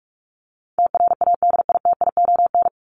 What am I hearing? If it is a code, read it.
Morse code, heard as a keyed tone, about 44 wpm, 700 Hz. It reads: TRADITION
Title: Top 1251-1500 Words 45wpm